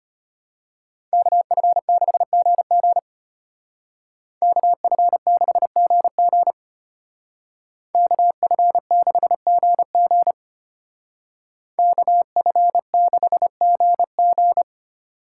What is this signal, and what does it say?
Morse code, heard as a keyed tone, about 30 wpm, 700 Hz. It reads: KF6GG KF6GG KF6GG KF6GG
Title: Speed Racing - USA Call Signs 25wpm